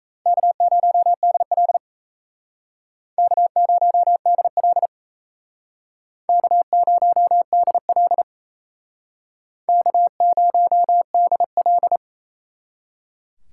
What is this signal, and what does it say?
Morse code, heard as a keyed tone, about 36 wpm, 700 Hz. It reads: K0DL K0DL K0DL K0DL